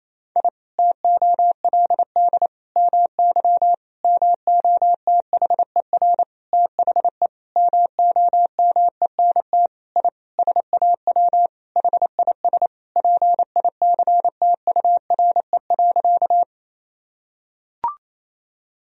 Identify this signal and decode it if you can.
Morse code, heard as a keyed tone, about 28 wpm, 700 Hz. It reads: I TOLD MY MOTHER THE MOMENT I SAW HIS PICTURE.
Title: Sentences from Top 1000 Words - Review Part 1 - 28wpm